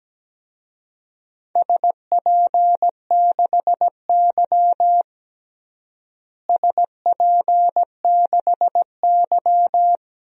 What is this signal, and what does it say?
Morse code, heard as a keyed tone, about 17 wpm, 700 Hz. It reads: SP6Y SP6Y
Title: DXCC-100 Call Signs - 4 Characters - Repeated 3x - 17wpm